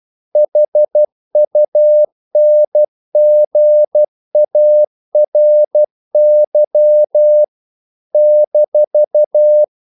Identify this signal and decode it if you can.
Morse code, heard as a keyed tone, about 12 wpm, 600 Hz. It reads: HUNGARY -
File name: CW / Practice / NationalCapitals12